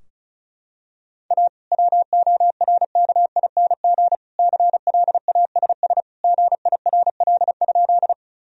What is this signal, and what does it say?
Morse code, heard as a keyed tone, about 36 wpm, 700 Hz. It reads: A WORKING CLASS GIRL?
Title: Sentences from Top 1000 Words - Review Part 4 - 35wpm